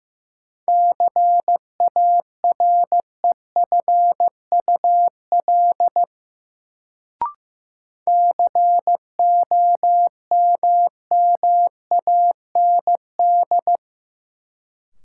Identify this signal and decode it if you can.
Morse code, heard as a keyed tone, about 15 wpm, 700 Hz. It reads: CAREFUL COMMAND
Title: Sentences from Top 1600 Words - New Word Drill - 15wpm